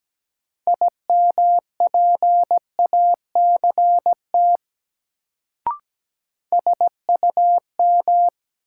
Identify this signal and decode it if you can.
Morse code, heard as a keyed tone, about 17 wpm, 700 Hz. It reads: IMPACT SUM
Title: Sentences from Top 1000 Words - Review Part 6 - New Word Drill - 17wpm